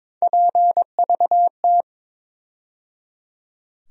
Morse code, heard as a keyed tone, about 23 wpm, 700 Hz. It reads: PVT
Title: USA Call Sign Postfixes - 3 Letters 22wpm